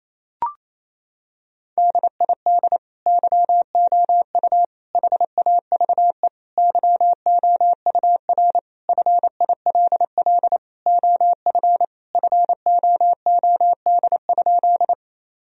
Morse code, heard as a keyed tone, about 28 wpm, 700 Hz. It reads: DID YOU HAVE YOUR FILL OF FOOD?
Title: Sentences from Top 1000 Words - Review Part 3 - 28wpm